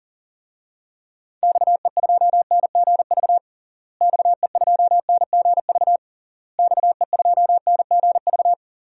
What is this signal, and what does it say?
Morse code, heard as a keyed tone, about 40 wpm, 700 Hz. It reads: XE2NGV XE2NGV XE2NGV
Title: DXCC-100 Call Signs - 6 Characters - Repeated 3x - 40wpm